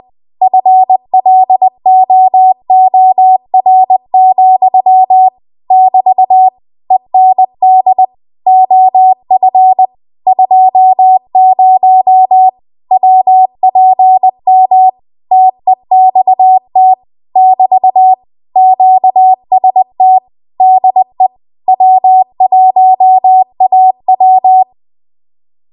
Morse code, heard as a keyed tone, about 20 wpm, 750 Hz. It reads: FLOOR, <BT> END OF 20 WPM TEXT <BT> QST DE W1AW
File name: Morse / 20WPM